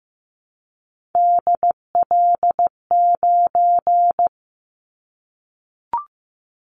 Morse code, heard as a keyed tone, about 15 wpm, 700 Hz. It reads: DL9
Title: Callsign Prefixes for Top 25 DXCC - Repeated 3x - 15wpm